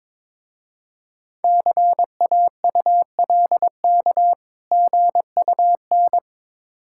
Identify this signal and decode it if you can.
Morse code, heard as a keyed tone, about 22 wpm, 690 Hz. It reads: CAULK GUN
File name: Difficult Words with Hints - 22wpm